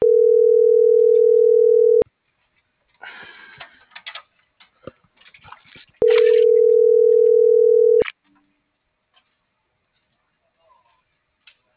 Ringback tone (North America), 440 + 480 Hz.
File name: CENTRAL SKYBLUE / out lopez.brandon